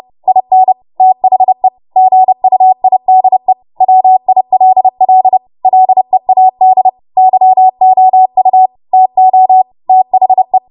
Morse code, heard as a keyed tone, about 30 wpm, 750 Hz. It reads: IN THE GUIDE WILL LEAD YOU TO THE
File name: Morse / 30WPM